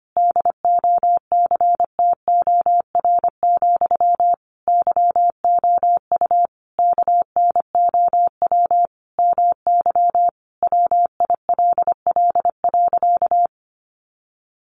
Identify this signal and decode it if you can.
Morse code, heard as a keyed tone, about 25 wpm, 700 Hz. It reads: DOCTOR, YOU KNOW MY WILL.
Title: Sentences from Top 1500 Words 25wpm